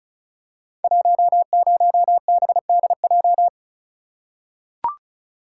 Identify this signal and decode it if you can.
Morse code, heard as a keyed tone, about 36 wpm, 690 Hz. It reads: 10BDJ
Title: Code Groups - 5 Letter:Numbers - 35wpm